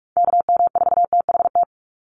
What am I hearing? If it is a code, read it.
Morse code, heard as a keyed tone, about 44 wpm, 700 Hz. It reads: KM4THT
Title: USA Call Signs 45wpm